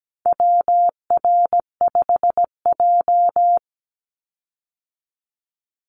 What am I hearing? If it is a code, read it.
Morse code, heard as a keyed tone, about 17 wpm, 700 Hz. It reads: WR5J
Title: Call Signs Repeated 2x - 17wpm